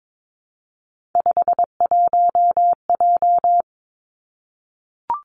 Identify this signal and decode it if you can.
Morse code, heard as a keyed tone, about 22 wpm, 700 Hz. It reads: 51J
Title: Three Challenging Letter-Number Combinations 22wpm